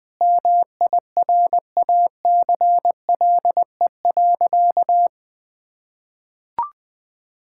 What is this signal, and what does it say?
Morse code, heard as a keyed tone, about 20 wpm, 700 Hz. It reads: MIRACLE.
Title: Sentences with 7-Word Phrases - 20wpm